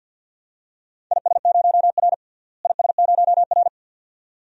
Morse code, heard as a keyed tone, about 53 wpm, 700 Hz. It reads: IS0R IS0R
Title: DXCC-100 Call Signs - 4 Characters - Repeated 2x - 50wpm